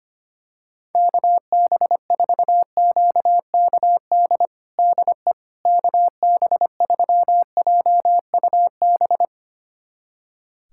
Morse code, heard as a keyed tone, about 25 wpm, 700 Hz. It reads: KB4QKD DE KB3JUB
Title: USA Callsign DE Callsign - 6 Characters - 25wpm